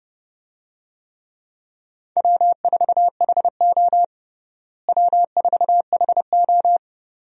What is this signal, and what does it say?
Morse code, heard as a keyed tone, about 30 wpm, 700 Hz. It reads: W4HO W4HO
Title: USA Call Signs - 4 Characters - Repeated 2x - 30wpm